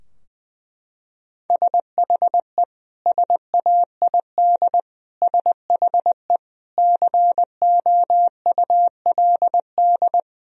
Morse code, heard as a keyed tone, about 21 wpm, 700 Hz. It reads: SHE SAID SHE COULD
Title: Sentences from Top 1000 Words - Review Part 4 - 20wpm